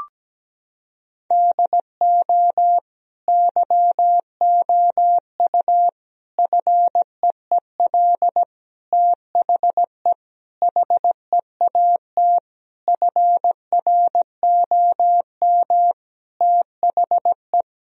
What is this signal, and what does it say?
Morse code, heard as a keyed tone, about 17 wpm, 700 Hz. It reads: DO YOU FEEL THE HEAT FROM THE
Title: Sentences from Top 1000 Words - Review Part 3 - 17wpm